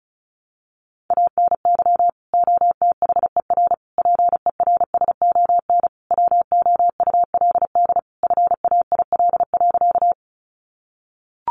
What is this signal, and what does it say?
Morse code, heard as a keyed tone, about 34 wpm, 700 Hz. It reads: ANY OTHER PERSON WOULD FAIL.
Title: Sentences from Top 1100 Words 35wpm